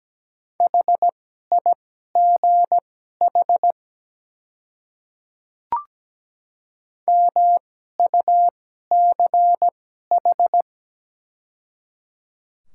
Morse code, heard as a keyed tone, about 17 wpm, spaced as for 12 wpm, 700 Hz. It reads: HIGH MUCH
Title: 2x Character Spacing - Top 200 Words - 17wpm